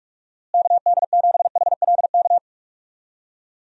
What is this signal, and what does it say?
Morse code, heard as a keyed tone, about 41 wpm, 700 Hz. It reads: KD7HLK
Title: USA Call Signs - 6 Characters - Repeated 2x - 45wpm